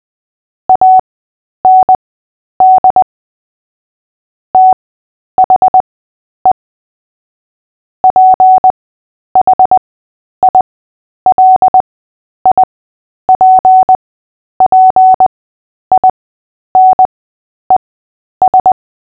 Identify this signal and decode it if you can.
Morse code, heard as a keyed tone, about 20 wpm, spaced as for 10 wpm, 750 Hz. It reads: AND THE PHILIPPINES
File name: cw / fists / Short-Story-3-10wpm